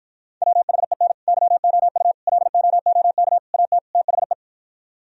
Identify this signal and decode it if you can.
Morse code, heard as a keyed tone, about 52 wpm, 700 Hz. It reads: WHEN YOU LOOK AT THE